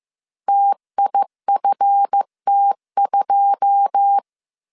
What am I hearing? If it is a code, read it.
Morse code, heard as a keyed tone, about 15 wpm, 800 Hz. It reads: TIFT2
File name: CW / DET / 1 9 1500C S C 14wpm 3245